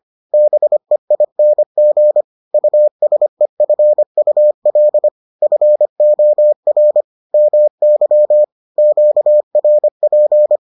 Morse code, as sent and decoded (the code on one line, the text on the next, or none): -... .. -. --. / ..- ... ..-. ..- .-.. / ..-. --- .-. / -- -.-- / --.- .-. .--.
BEING USEFUL FOR MY QRP